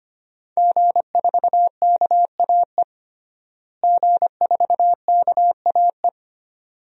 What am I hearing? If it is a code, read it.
Morse code, heard as a keyed tone, about 25 wpm, 700 Hz. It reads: G4KAE G4KAE
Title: DXCC-100 Call Signs - 5 Characters - Repeated 2x - 25wpm